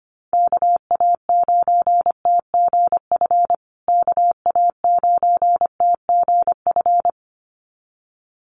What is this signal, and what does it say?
Morse code, heard as a keyed tone, about 25 wpm, 700 Hz. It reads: KA9TGF KA9TGF
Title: USA Call Signs Repeated 2x 25wpm